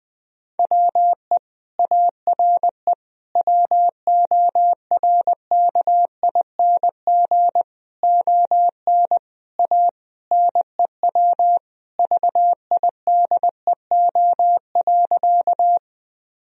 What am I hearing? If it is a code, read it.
Morse code, heard as a keyed tone, about 21 wpm, 700 Hz. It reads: WE ARE WORKING ON A NEW VIDEO.
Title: Sentences from Top 1000 Words - Review Part 4 - 20wpm